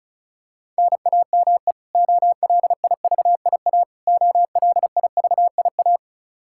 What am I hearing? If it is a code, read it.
Morse code, heard as a keyed tone, about 36 wpm, 700 Hz. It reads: NAME OLIVIA OLIVIA
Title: Practice QSO - Exchange 1 - Beginning - 35wpm